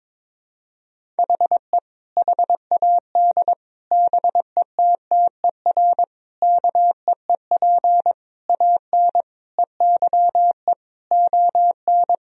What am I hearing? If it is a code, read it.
Morse code, heard as a keyed tone, about 21 wpm, 700 Hz. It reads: HE HAD BETTER KEEP AN EYE ON
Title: Sentences from Top 1000 Words - Review Part 1 - 22wpm